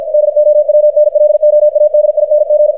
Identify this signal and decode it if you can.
Morse code, heard as a keyed tone, about 52 wpm, 600 Hz, with radio noise. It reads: DOWNLOADING
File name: archived / CRSnet